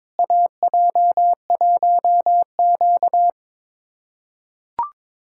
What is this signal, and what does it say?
Morse code, heard as a keyed tone, about 22 wpm, 700 Hz. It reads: AJ1Q
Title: USA Callsign DE Callsign - 4 Characters - 22wpm